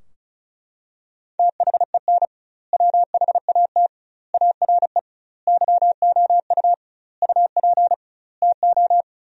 Morse code, heard as a keyed tone, about 35 wpm, 700 Hz. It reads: THEN WHAT ARE YOU UP TO
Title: 2x Word-Spacing - Sentences from Top 100 Words - 35wpm